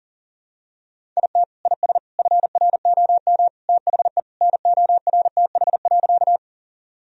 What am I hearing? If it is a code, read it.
Morse code, heard as a keyed tone, about 40 wpm, 700 Hz. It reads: IT IS FROM THE NORTH.
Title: Sentences from Top 1000 Words - Review Part 1 - 40wpm